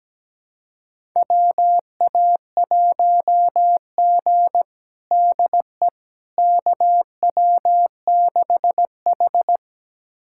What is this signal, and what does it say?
Morse code, heard as a keyed tone, about 17 wpm, 700 Hz. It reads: WA1G DE KW6H
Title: USA Callsign DE Callsign - 4 Characters - 17wpm